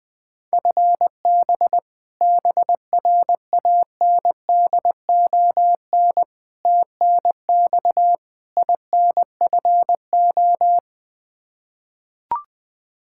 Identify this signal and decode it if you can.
Morse code, heard as a keyed tone, about 21 wpm, 700 Hz. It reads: FB BRANDON TNX INFO
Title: Practice QSO - Exchange 3 - More Info - 20wpm